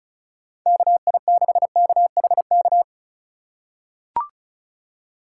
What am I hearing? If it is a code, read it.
Morse code, heard as a keyed tone, about 36 wpm, 700 Hz. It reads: KI6KHK
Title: USA Call Signs - 6 Characters - 3x Character Spacing - 35wpm